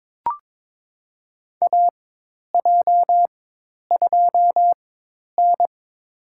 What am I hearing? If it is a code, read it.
Morse code, heard as a keyed tone, about 21 wpm, spaced as for 10 wpm, 700 Hz. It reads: AJ2N
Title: USA Call Signs - 4 Characters - 4x Character Spacing - 22wpm